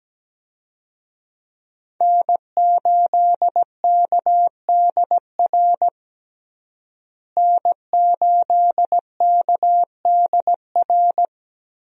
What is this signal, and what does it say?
Morse code, heard as a keyed tone, about 17 wpm, 700 Hz. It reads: N8KDR N8KDR
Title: USA Call Signs - 5 Characters - Repeated 3x - 17wpm